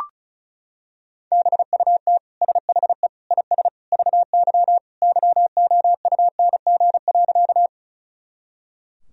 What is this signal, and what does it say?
Morse code, heard as a keyed tone, about 36 wpm, 700 Hz. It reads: BUT SHE IS VY YOUNG.
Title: Sentences from Top 400 Words and Abbreviations 35wpm